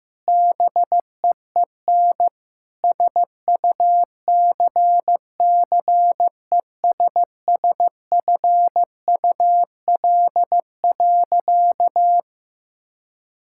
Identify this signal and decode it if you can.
Morse code, heard as a keyed tone, about 15 wpm, 700 Hz. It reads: BEEN SUCCESSFUL.
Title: Sentences from Top 1600 Words 15wpm